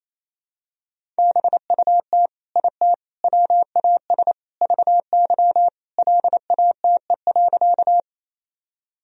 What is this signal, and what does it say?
Morse code, heard as a keyed tone, about 28 wpm, 700 Hz. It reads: BUT IT WAS VY LATE.
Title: Sentences from Top 300 Words and Abbreviations 28wpm